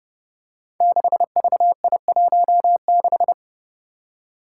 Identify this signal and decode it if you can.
Morse code, heard as a keyed tone, about 30 wpm, 700 Hz. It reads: 6VI16
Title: Five Challenging Letter-Number Combinations 30wpm